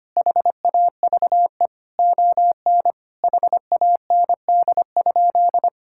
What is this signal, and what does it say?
Morse code, heard as a keyed tone, about 25 wpm, 700 Hz. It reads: HAVE ON HAND?
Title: Sentences from Top 1000 Words - Review Part 5 - 25wpm